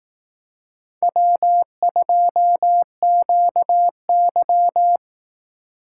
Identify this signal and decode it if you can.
Morse code, heard as a keyed tone, about 18 wpm, 700 Hz. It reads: W2QY
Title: Speed Racing - USA 1x2 Call Signs 15wpm